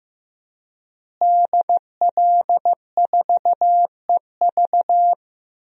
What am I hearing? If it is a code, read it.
Morse code, heard as a keyed tone, about 15 wpm, 700 Hz. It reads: DL4EV
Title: DXCC-100 Call Signs - 5 Characters - Repeated 3x - 15wpm